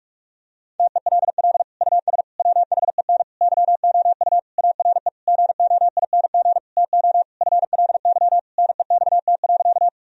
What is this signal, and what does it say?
Morse code, heard as a keyed tone, about 46 wpm, 700 Hz. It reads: TELL US WHEN YOU ARE GOING TO FLY NEXT.
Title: Sentences from Top 1000 Words - Review Part 2 - 45wpm